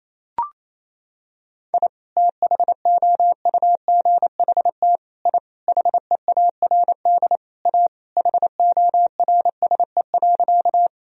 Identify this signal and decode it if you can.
Morse code, heard as a keyed tone, about 28 wpm, 700 Hz. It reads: I THOUGHT I HEARD A HORSE.
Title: Sentences from Top 1000 Words - Review Part 1 - 28wpm